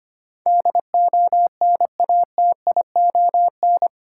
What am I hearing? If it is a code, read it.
Morse code, heard as a keyed tone, about 25 wpm, 700 Hz. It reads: DONATION
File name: Eight Letter Words - 25wpm